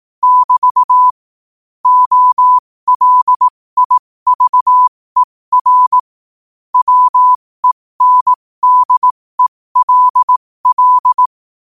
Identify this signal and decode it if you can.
Morse code, heard as a keyed tone, about 18 wpm, spaced as for 15 wpm, 1000 Hz. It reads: <BT> OLIVER WENDELL
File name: qotdfiles / 15wpm